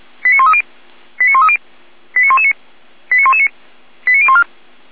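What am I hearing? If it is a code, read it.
Five-tone selective call (ZVEI), heard as quick runs of five tones, tones of about 70 ms. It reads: 89130 89130 89119 89119 89014
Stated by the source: ZVEI five-tone selective call received on an SDR.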